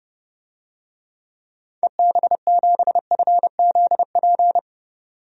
Morse code, heard as a keyed tone, about 32 wpm, 700 Hz. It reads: EB7FZP
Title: DXCC-100 Call Signs - 6 Characters - 30wpm